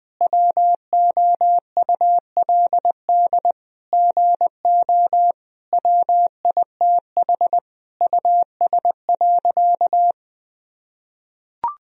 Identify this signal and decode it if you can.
Morse code, heard as a keyed tone, about 21 wpm, 700 Hz. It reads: WOULD GO WITH US.
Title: Sentences from Top 1000 Words - Review Part 3 - 20wpm